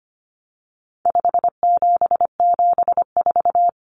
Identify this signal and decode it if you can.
Morse code, heard as a keyed tone, about 25 wpm, 700 Hz. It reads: 5774
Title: Common Cut Numbers - 4-Digit 25wpm